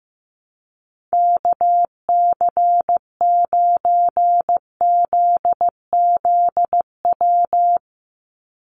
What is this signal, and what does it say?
Morse code, heard as a keyed tone, about 15 wpm, 700 Hz. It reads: KC9ZZW
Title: USA Call Signs 15wpm